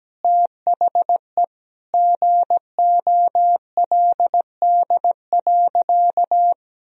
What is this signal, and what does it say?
Morse code, heard as a keyed tone, about 17 wpm, 700 Hz. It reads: THE GOLD.
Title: Sentences from Top 1000 Words - Review Part 3 - 17wpm